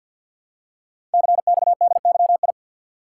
Morse code, heard as a keyed tone, about 48 wpm, 690 Hz. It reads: CXDYI
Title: Code Groups - 5 Letters - 50wpm